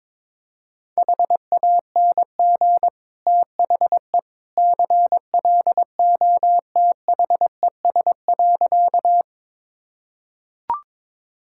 Morse code, heard as a keyed tone, about 23 wpm, 700 Hz. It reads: HANG THE CLOTHES.
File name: practice / 22wpm / Easy Sentences with Hints - 22wpm